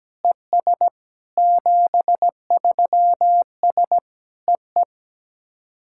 Morse code, heard as a keyed tone, about 17 wpm, 700 Hz. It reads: ES 73S EE
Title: SOTA QSO Practice 17wpm